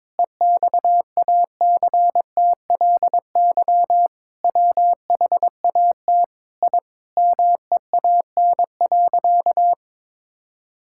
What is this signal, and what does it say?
Morse code, heard as a keyed tone, about 22 wpm, 700 Hz. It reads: EXACTLY WHAT I MEAN.
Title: Sentences from Top 1000 Words - Review Part 6 - 22wpm